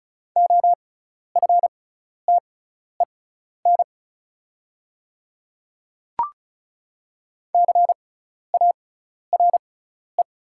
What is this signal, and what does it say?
Morse code, heard as a keyed tone, about 34 wpm, spaced as for 12 wpm, 700 Hz. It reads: OFTEN CARE
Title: 6x Character Spacing - Top 300 Words - 35wpm